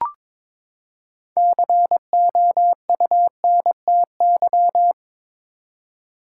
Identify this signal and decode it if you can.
Morse code, heard as a keyed tone, about 21 wpm, 700 Hz. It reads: COUNTY
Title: Six Letter Words - 22wpm